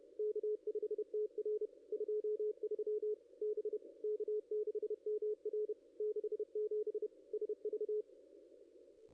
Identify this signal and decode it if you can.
Morse code, heard as a keyed tone, about 32 wpm, 440 Hz. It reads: K5TR 23 B K6MR 67 SV
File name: contests / ss-cw-k5tr-op-n5rz -0400